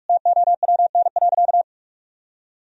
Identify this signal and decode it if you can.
Morse code, heard as a keyed tone, about 46 wpm, 700 Hz. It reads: TOWN.
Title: Sentences from Top 1000 Words - Review Part 4 - 45wpm